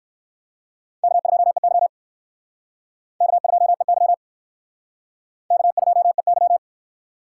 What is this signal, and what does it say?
Morse code, heard as a keyed tone, about 57 wpm, 700 Hz. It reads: K2EY K2EY K2EY